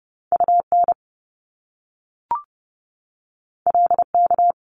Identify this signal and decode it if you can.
Morse code, heard as a keyed tone, about 30 wpm, 700 Hz. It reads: UN LK
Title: Top Two Letter Combinations in Words 30wpm